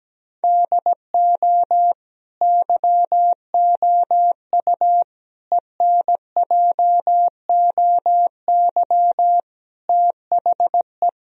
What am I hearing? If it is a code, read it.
Morse code, heard as a keyed tone, about 17 wpm, 700 Hz. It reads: DO YOU ENJOY THE
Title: Sentences from Top 1000 Words - Review Part 4 - 17wpm